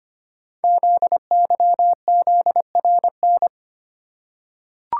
Morse code, heard as a keyed tone, about 25 wpm, 700 Hz. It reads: ZYZRN
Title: Code Groups - 5 Letters - 25wpm